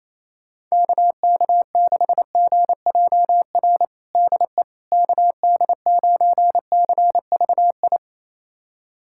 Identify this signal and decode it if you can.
Morse code, heard as a keyed tone, about 28 wpm, 700 Hz. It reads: KK6GJR DE KD9CVI
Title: USA Callsign DE Callsign - 6 Characters - 28wpm